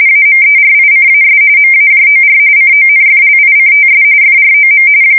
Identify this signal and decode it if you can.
RTTY, about 45.45 baud. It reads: AMERICAS MAJOR CONFLICTS FROM
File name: fsk170 / USS-New-Jersey